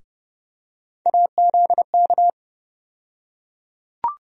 Morse code, heard as a keyed tone, about 30 wpm, 700 Hz. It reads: AZK
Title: USA Call Sign Postfixes - 3 Letters 30wpm